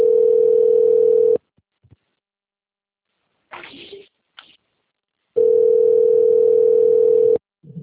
Ringback tone (North America), 440 + 480 Hz.